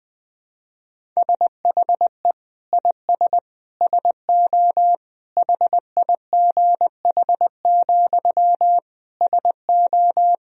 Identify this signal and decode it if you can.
Morse code, heard as a keyed tone, about 20 wpm, 700 Hz. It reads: SHE IS SO HIGH, SO
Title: Sentences from Top 1000 Words - Review Part 1 - 20wpm